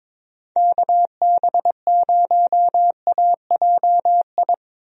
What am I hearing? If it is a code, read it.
Morse code, heard as a keyed tone, about 22 wpm, 700 Hz. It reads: KB0AJI
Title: USA Call Signs - 6 Characters - 22wpm